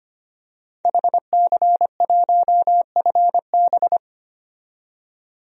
Morse code, heard as a keyed tone, about 25 wpm, 700 Hz. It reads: HC1FB
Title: DXCC-100 Call Signs - 5 Characters - 25wpm